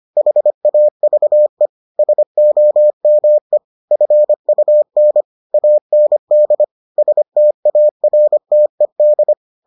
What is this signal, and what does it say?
Morse code, heard as a keyed tone, about 25 wpm, 600 Hz. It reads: HAVE SOME FUN AND STARTED